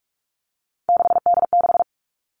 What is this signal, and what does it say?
Morse code, heard as a keyed tone, about 44 wpm, 700 Hz. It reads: 6D6
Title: All Cut Numbers - 3-Digit 45wpm